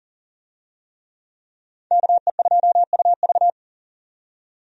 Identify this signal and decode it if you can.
Morse code, heard as a keyed tone, about 39 wpm, 700 Hz. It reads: KE2UV
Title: USA Call Signs - 5 Characters - 40wpm